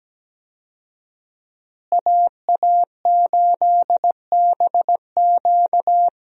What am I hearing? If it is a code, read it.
Morse code, heard as a keyed tone, about 17 wpm, 700 Hz. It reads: AA8BQ
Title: USA Call Signs - 5 Characters - Repeated 3x - 17wpm